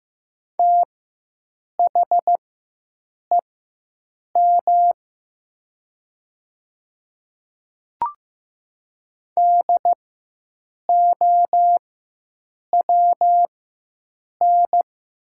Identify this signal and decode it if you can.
Morse code, heard as a keyed tone, about 15 wpm, spaced as for 7 wpm, 700 Hz. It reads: THEM DOWN
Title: 4x Character Spacing - Top 100 Words - 15wpm